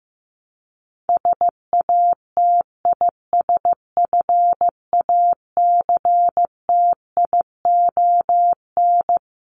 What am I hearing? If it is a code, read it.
Morse code, heard as a keyed tone, about 15 wpm, 700 Hz. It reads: SATISFACTION
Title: Top 1751-2000 Words 15wpm